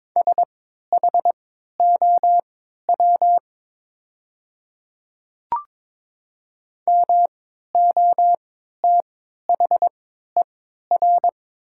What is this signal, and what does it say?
Morse code, heard as a keyed tone, about 22 wpm, spaced as for 13 wpm, 700 Hz. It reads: SHOW MOTHER
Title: 3x Character Spacing - Top 300 Words - 22wpm